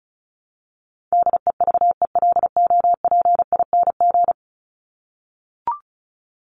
Morse code, heard as a keyed tone, about 34 wpm, 700 Hz. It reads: DEVELOPING
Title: ING Suffix - Top 100 Words with Base Words 35wpm